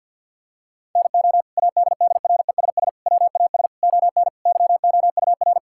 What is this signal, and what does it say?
Morse code, heard as a keyed tone, about 52 wpm, 700 Hz. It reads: NO ADDRESS WAS ON YOUR
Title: Sentences from Top 1000 Words - Review Part 6 - 50wpm